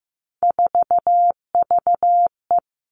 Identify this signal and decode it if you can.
Morse code, heard as a keyed tone, about 15 wpm, 700 Hz. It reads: 4VE
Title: Three Challenging Letter-Number Combinations 15wpm